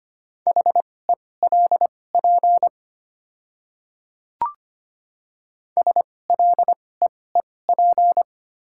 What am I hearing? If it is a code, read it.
Morse code, heard as a keyed tone, about 25 wpm, spaced as for 18 wpm, 700 Hz. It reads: HELP SLEEP
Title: 2x Character Spacing - Top 500 Words - 25wpm